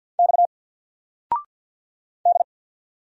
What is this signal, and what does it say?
Morse code, heard as a keyed tone, about 50 wpm, 700 Hz. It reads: X D
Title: Single Letter-Number 50wpm